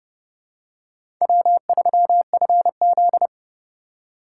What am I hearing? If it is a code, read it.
Morse code, heard as a keyed tone, about 30 wpm, 700 Hz. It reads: W3FZ
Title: USA Call Signs - 4 Characters - 8x Character Spacing - 30wpm